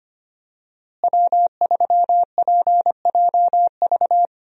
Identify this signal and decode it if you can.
Morse code, heard as a keyed tone, about 25 wpm, 700 Hz. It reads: W3PJV
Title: USA Call Signs - 5 Characters - Repeated 3x - 25wpm